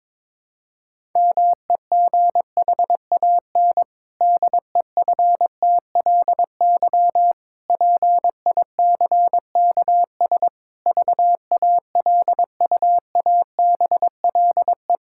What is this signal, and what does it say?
Morse code, heard as a keyed tone, about 22 wpm, 700 Hz. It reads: MEGHAN DEFTLY PICKS VALUABLE
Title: Pangram Challenge v2 - 22wpm